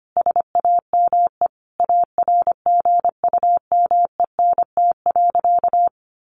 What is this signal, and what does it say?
Morse code, heard as a keyed tone, about 25 wpm, 700 Hz. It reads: SAME ARGUMENT.
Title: Sentences from Top 1400 Words 25wpm